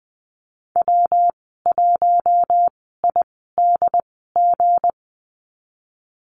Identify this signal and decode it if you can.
Morse code, heard as a keyed tone, about 20 wpm, spaced as for 15 wpm, 700 Hz. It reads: W1IDG
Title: 167 Lesson 27g - Number 7 - Call Signs 20wpm